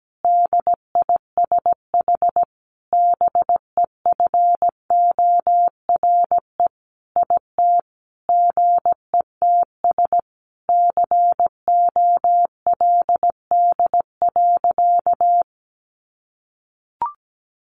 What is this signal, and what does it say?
Morse code, heard as a keyed tone, about 17 wpm, 700 Hz. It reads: DISH BEFORE IT GETS COLD.
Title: Sentences from Top 1300 Words 17wpm